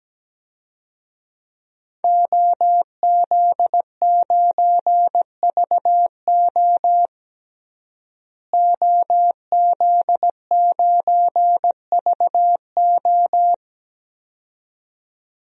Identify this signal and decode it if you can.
Morse code, heard as a keyed tone, about 17 wpm, 700 Hz. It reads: OZ9VO OZ9VO
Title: DXCC-100 Call Signs - 5 Characters - Repeated 2x - 17wpm